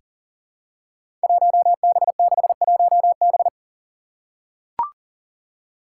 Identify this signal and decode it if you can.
Morse code, heard as a keyed tone, about 37 wpm, 700 Hz. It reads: 1B61B
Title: Five Challenging Letter-Number Combinations 40wpm